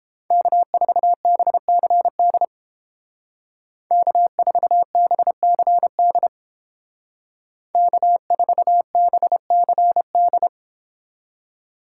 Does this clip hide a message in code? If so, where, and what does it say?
Yes: Morse code from 0.3 to 10.5 s, heard as a keyed tone, about 31 wpm, 700 Hz. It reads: K4BCD K4BCD K4BCD